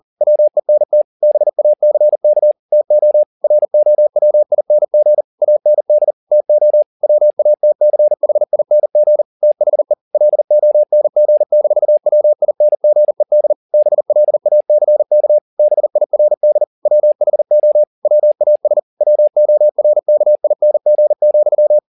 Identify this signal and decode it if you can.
Morse code, heard as a keyed tone, about 40 wpm, 600 Hz. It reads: WENT BACK TO ROWING AND TO WATCHING THE LONG-WINGED BLACK BIRD WHO WAS WORKING,